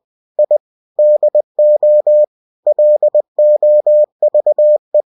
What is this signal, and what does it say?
Morse code, heard as a keyed tone, about 20 wpm, 600 Hz. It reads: I DO LOVE
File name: cw-books / winnie-the-pooh / winnie-20wpm-page-0044